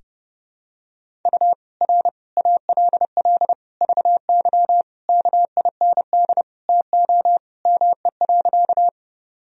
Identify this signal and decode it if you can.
Morse code, heard as a keyed tone, about 32 wpm, 700 Hz. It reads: U R ALL VY KIND TO ME.
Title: Sentences from Top 200 Words and Abbreviations 30wpm